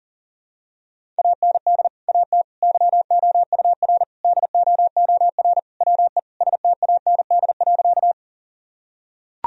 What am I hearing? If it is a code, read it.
Morse code, heard as a keyed tone, about 40 wpm, 700 Hz. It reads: AND AT YOUR DOOR WE STAND.
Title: Sentences from Top 1000 Words - Review Part 1 - 40wpm